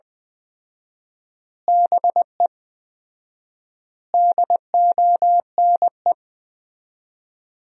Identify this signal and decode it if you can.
Morse code, heard as a keyed tone, about 21 wpm, 700 Hz. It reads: BE DONE
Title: 4x Word-Spacing - Sentences from Top 500 Words - 20wpm